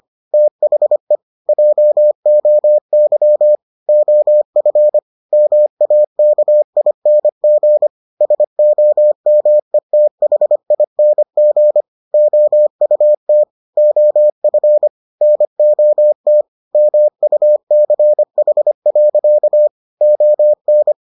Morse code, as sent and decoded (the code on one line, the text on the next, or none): - .... / .--- --- -.-- / --- ..-. / -- .- -.- .. -. --. / ... --- -- . - .... .. -. --. / --- ..- - / --- ..-. / -. --- - / -- ..- -.-. .... .-.-.- / --- -.
THE JOY OF MAKING SOMETHING OUT OF NOT MUCH. ON